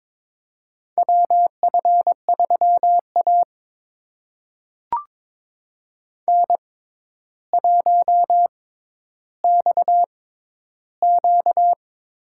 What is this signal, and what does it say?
Morse code, heard as a keyed tone, about 22 wpm, spaced as for 11 wpm, 700 Hz. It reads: WF3A N1XQ
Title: USA Call Signs - 4 Characters - 6x Character Spacing - 22wpm